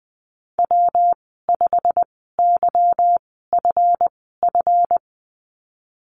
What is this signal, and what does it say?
Morse code, heard as a keyed tone, about 20 wpm, spaced as for 15 wpm, 700 Hz. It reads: W5YFF
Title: 193 Lesson 31g - Letter v - Call Signs 20wpm